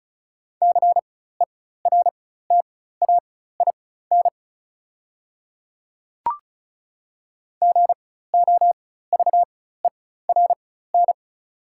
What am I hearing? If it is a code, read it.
Morse code, heard as a keyed tone, about 36 wpm, spaced as for 16 wpm, 690 Hz. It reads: CERTAIN GOVERN
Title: 4x Character Spacing - Top 500 Words - 35wpm